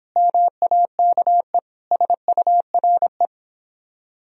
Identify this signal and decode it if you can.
Morse code, heard as a keyed tone, about 24 wpm, 700 Hz. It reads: MAKE SURE